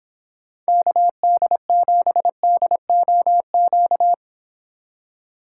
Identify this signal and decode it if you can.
Morse code, heard as a keyed tone, about 30 wpm, 700 Hz. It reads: KD7DOQ